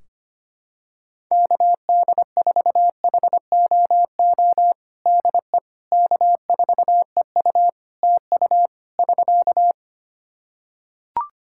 Morse code, heard as a keyed tone, about 25 wpm, 700 Hz. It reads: KD4HOO DE K4EU TU <SK>
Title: Practice QSO - Exchange 4 - Ending - 25wpm